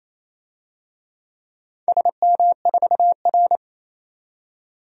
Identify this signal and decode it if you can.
Morse code, heard as a keyed tone, about 28 wpm, 700 Hz. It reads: SM4R
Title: DXCC-100 Call Signs - 4 Characters - 28wpm